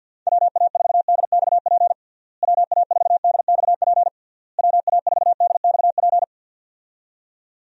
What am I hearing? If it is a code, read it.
Morse code, heard as a keyed tone, about 52 wpm, 700 Hz. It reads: WA4DXP WA4DXP WA4DXP
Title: USA Call Signs - 6 Characters - Repeated 3x - 50wpm